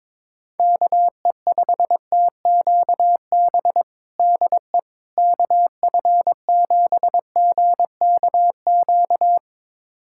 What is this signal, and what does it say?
Morse code, heard as a keyed tone, about 22 wpm, 700 Hz. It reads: KE5TQB DE KF7GKQ
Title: USA Callsign DE Callsign - 6 Characters - 22wpm